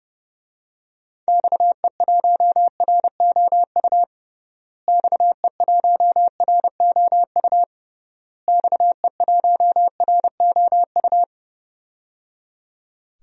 Morse code, heard as a keyed tone, about 30 wpm, 700 Hz. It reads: XE1ROU XE1ROU XE1ROU
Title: DXCC-100 Call Signs - 6 Characters - Repeated 3x - 30wpm